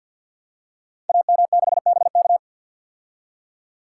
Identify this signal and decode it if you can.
Morse code, heard as a keyed tone, about 52 wpm, 700 Hz. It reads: AM6BK
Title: Code Groups - 5 Letter:Numbers - 50wpm